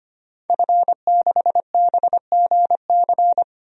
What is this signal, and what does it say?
Morse code, heard as a keyed tone, about 25 wpm, 700 Hz. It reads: F6BGC
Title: DXCC-100 Call Signs - 5 Characters - 25wpm